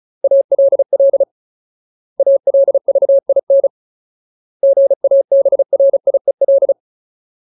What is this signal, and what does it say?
Morse code, heard as a keyed tone, about 34 wpm, 550 Hz. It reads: ALL ALVIN GABRIEL
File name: Ultimate Word List / 35WPM Ultimate